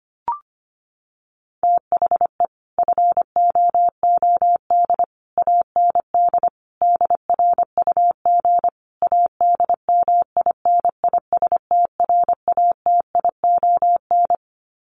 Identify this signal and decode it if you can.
Morse code, heard as a keyed tone, about 25 wpm, 700 Hz. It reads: THE FOOD AND DRUG ADMINISTRATION
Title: Sets of 5 Words 25wpm